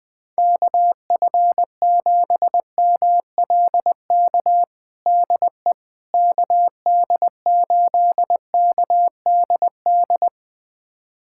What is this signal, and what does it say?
Morse code, heard as a keyed tone, about 20 wpm, 700 Hz. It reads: KF7MLK DE KD8KDD
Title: USA Callsign DE Callsign 20wpm